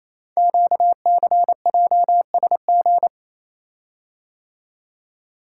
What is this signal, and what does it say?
Morse code, heard as a keyed tone, about 27 wpm, 700 Hz. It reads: QCJSG
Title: Code Groups - 5 Letters - 28wpm